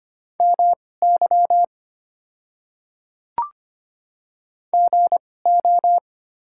Morse code, heard as a keyed tone, about 25 wpm, spaced as for 18 wpm, 700 Hz. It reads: MY GO
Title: 2x Character Spacing - Top 100 Words - 25wpm